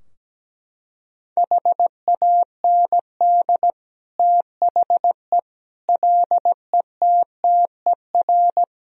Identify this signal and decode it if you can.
Morse code, heard as a keyed tone, about 17 wpm, 700 Hz. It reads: HAND THE LETTER
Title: Sentences from Top 1000 Words - Review Part 2 - 17wpm